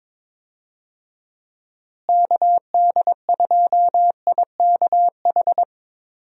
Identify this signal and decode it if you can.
Morse code, heard as a keyed tone, about 22 wpm, 700 Hz. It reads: KD2IKH
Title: USA Call Signs - 6 Characters - Repeated 2x - 22wpm